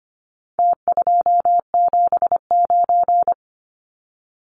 Morse code, heard as a keyed tone, about 25 wpm, 700 Hz. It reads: T279
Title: Numbers - Contest Style - 4-Digit 25wpm